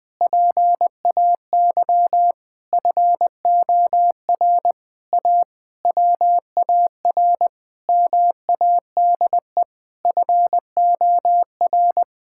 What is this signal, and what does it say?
Morse code, heard as a keyed tone, about 20 wpm, 700 Hz. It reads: PAY FOR A WAR MADE FOR
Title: Sentences from Top 1000 Words - Review Part 6 - 20wpm